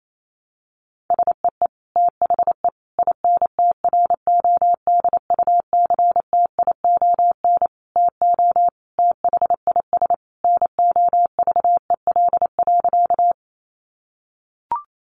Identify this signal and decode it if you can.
Morse code, heard as a keyed tone, about 28 wpm, 700 Hz. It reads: SEE THE INTRODUCTION TO THIS NOVEL.
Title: Sentences from Top 1300 Words 28wpm